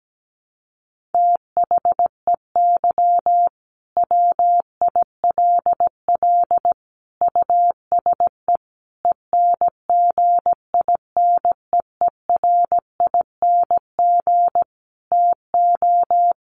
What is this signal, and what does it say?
Morse code, heard as a keyed tone, about 17 wpm, 700 Hz. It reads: THEY WILL USE ENGINEERING TO
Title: Sentences from Top 1500 Words 17wpm